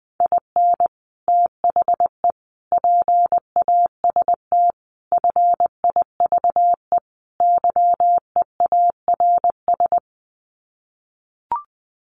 Morse code, heard as a keyed tone, about 20 wpm, 700 Hz. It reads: IN THE PAST FIVE YEARS
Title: Sets of 5 Words Encore 20wpm